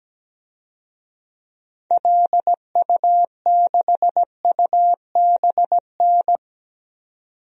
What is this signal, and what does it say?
Morse code, heard as a keyed tone, about 17 wpm, 700 Hz. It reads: LU6UBN
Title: DXCC-100 Call Signs - 6 Characters - 17wpm